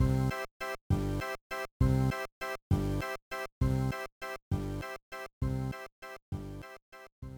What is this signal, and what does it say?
There is no code.